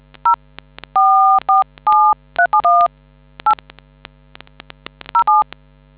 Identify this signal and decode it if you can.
DTMF, heard as telephone touch tones, tones of about 140 ms. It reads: 0 447371 8 07